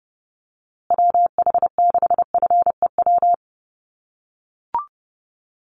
Morse code, heard as a keyed tone, about 30 wpm, 700 Hz. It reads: WH6FEW
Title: USA Call Signs Repeated 3x 30wpm